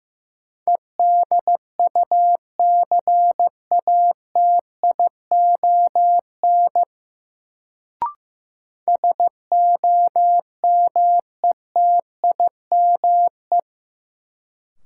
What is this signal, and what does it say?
Morse code, heard as a keyed tone, about 15 wpm, 700 Hz. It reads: EDUCATION SOMETIME
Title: Sentences from Top 1000 Words - Review Part 6 - New Word Drill - 15wpm